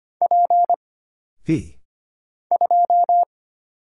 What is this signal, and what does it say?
Morse code, heard as a keyed tone, about 25 wpm, 700 Hz. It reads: P 2
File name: Single Character (Letter:Number:Punctuation:Prosign) - Mind-Melt - 25wpm